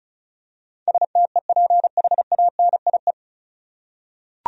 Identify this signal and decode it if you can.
Morse code, heard as a keyed tone, about 36 wpm, 700 Hz. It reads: STEPHANIE
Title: USA Common Names - 35wpm